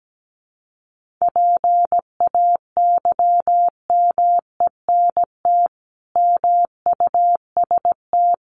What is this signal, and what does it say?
Morse code, heard as a keyed tone, about 17 wpm, 700 Hz. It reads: PAYMENT MUST
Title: Sentences from Top 1500 Words 17wpm